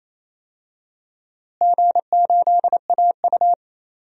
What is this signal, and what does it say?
Morse code, heard as a keyed tone, about 27 wpm, 700 Hz. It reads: G8AU
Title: DXCC-100 Call Signs - 4 Characters - Repeated 2x - 28wpm